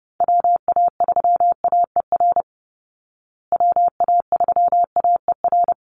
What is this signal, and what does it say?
Morse code, heard as a keyed tone, about 30 wpm, 700 Hz. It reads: WA3AER WA3AER
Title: Call Signs Repeated 2x - 30wpm